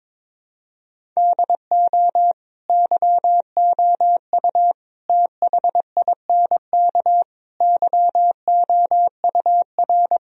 Morse code, heard as a keyed tone, about 22 wpm, 700 Hz. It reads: DO YOU THINK YOUR
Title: Sentences from Top 1000 Words - Review Part 1 - 22wpm